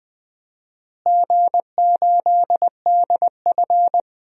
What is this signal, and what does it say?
Morse code, heard as a keyed tone, about 20 wpm, 700 Hz. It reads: G8DF
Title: DXCC-100 Call Signs - 4 Characters - Repeated 3x - 20wpm